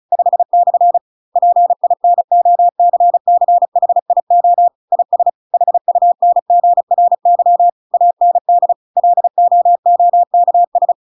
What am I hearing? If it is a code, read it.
Morse code, heard as a keyed tone, about 34 wpm, 700 Hz. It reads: 5/ PINOCCHIO IS HUNGRY AND LOOKS
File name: Pinocchio / Pinocchio ch5 35 35 ik1hge